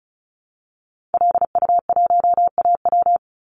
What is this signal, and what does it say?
Morse code, heard as a keyed tone, about 34 wpm, 700 Hz. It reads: LU1AW
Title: Call Signs Repeated 2x - 35wpm